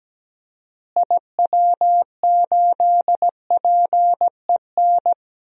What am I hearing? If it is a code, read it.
Morse code, heard as a keyed tone, about 17 wpm, 700 Hz. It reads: IW8PEN
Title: DXCC-100 Call Signs - 6 Characters - Repeated 2x - 17wpm